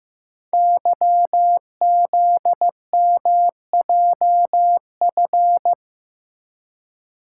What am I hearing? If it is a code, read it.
Morse code, heard as a keyed tone, about 15 wpm, 700 Hz. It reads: YZMJF
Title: Code Groups - 5 Letters - 15wpm